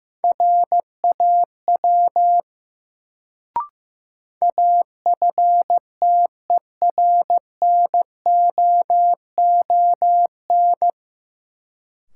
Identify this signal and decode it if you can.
Morse code, heard as a keyed tone, about 16 wpm, 700 Hz. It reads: RAW AFTERNOON
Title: Sentences from Top 1500 Words - New Word Drill Alt - 15wpm